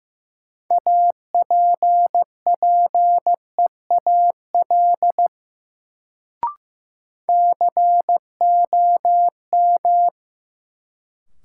Morse code, heard as a keyed tone, about 15 wpm, 700 Hz. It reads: APPEAL COM
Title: Sentences from Top 1300 Words - New Word Drill - 15wpm